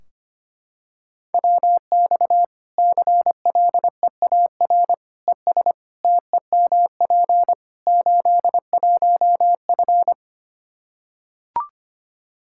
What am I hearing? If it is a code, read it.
Morse code, heard as a keyed tone, about 25 wpm, 700 Hz. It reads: WX CLEAR ES TEMP 81F
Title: Practice QSO - Exchange 2 - Info - 25wpm